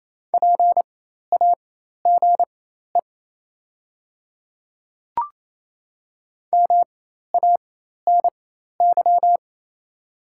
Morse code, heard as a keyed tone, about 28 wpm, spaced as for 13 wpm, 700 Hz. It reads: PAGE MANY
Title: 4x Character Spacing - Top 300 Words - 28wpm